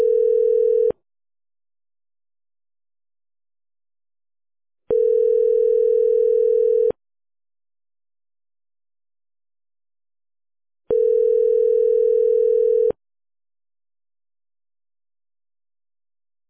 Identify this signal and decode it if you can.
Ringback tone (North America), 440 + 480 Hz.